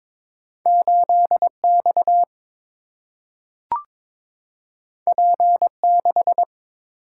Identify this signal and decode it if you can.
Morse code, heard as a keyed tone, about 22 wpm, 700 Hz. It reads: 8 X P 6
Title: Single-Letter Single-Number 22wpm